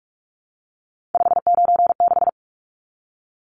Morse code, heard as a keyed tone, about 44 wpm, 700 Hz. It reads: 596
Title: Numbers - 3-Digit 45wpm